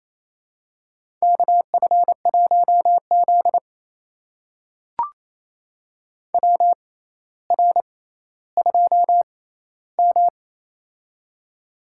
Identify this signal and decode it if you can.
Morse code, heard as a keyed tone, about 28 wpm, spaced as for 14 wpm, 700 Hz. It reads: KF1Z WR2M
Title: USA Call Signs - 4 Characters - 6x Character Spacing - 28wpm